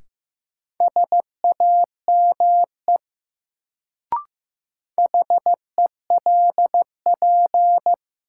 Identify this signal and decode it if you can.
Morse code, heard as a keyed tone, about 15 wpm, 700 Hz. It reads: SAME HELP
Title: Sentences from Top 200 Words - New Word Drill - 15wpm